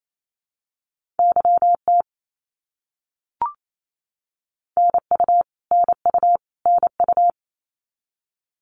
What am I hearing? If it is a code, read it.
Morse code, heard as a keyed tone, about 28 wpm, 700 Hz. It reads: YT NU NU NU
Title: Canadian Provinces and Territories - Repeated 3x - 28wpm